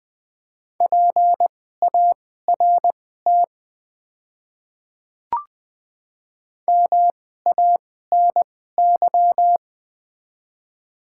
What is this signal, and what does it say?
Morse code, heard as a keyed tone, about 20 wpm, spaced as for 15 wpm, 700 Hz. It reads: PART MANY
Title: 2x Character Spacing - Top 200 Words - 20wpm